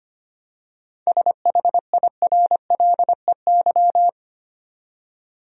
Morse code, heard as a keyed tone, about 25 wpm, 700 Hz. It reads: SHIRLEY
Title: USA Common Names - 25wpm